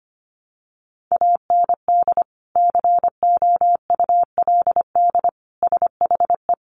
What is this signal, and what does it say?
Morse code, heard as a keyed tone, about 25 wpm, 700 Hz. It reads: AND COULD SHE
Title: Sentences from Top 100 Words 25wpm